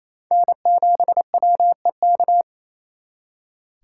Morse code, heard as a keyed tone, about 35 wpm, 700 Hz. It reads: N7WEK
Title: Speed Racing - USA 1x3 Call Signs 28wpm